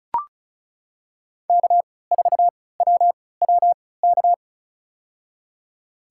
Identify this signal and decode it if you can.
Morse code, heard as a keyed tone, about 34 wpm, spaced as for 20 wpm, 700 Hz. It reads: K4WWK
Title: USA Call Signs - 5 Characters - 3x Character Spacing - 35wpm